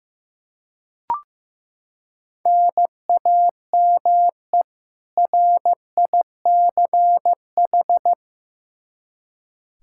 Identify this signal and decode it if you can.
Morse code, heard as a keyed tone, about 15 wpm, 700 Hz. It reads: NAME RICH
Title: K1USN SST QSO Practice 15wpm